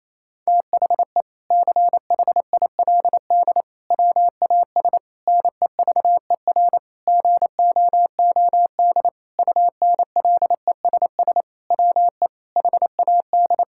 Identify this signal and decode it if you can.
Morse code, heard as a keyed tone, about 28 wpm, 700 Hz. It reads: THE CHILD WAS NEVER GOOD UNLESS WE HAD
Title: Sentences from Top 1000 Words - Review Part 6 - 28wpm